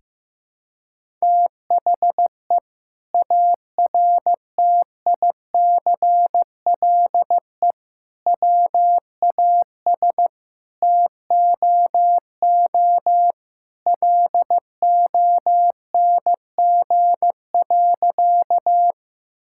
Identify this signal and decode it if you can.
Morse code, heard as a keyed tone, about 15 wpm, 700 Hz. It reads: THE ARTICLE WAS TOO LONG.
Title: Sentences from Top 1000 Words - Review Part 5 - 15wpm